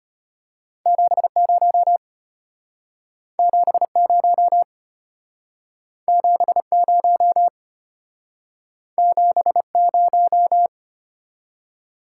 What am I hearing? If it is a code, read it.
Morse code, heard as a keyed tone, about 33 wpm, 700 Hz. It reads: 70 70 70 70